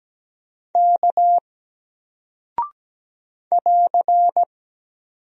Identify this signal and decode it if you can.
Morse code, heard as a keyed tone, about 17 wpm, 700 Hz. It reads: K <AR>
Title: Single Character (Letter:Number:Punctuation:Prosign) Alt 17wpm